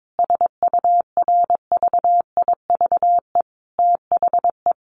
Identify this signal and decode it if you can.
Morse code, heard as a keyed tone, about 22 wpm, 700 Hz. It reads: SURVIVE THE
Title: Sentences from Top 1300 Words 22wpm